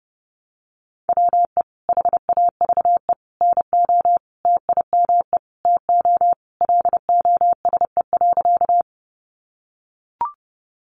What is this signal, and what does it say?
Morse code, heard as a keyed tone, about 30 wpm, 700 Hz. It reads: WE HAVE NO TIME TO LOSE.
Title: Sentences from Top 1100 Words 30wpm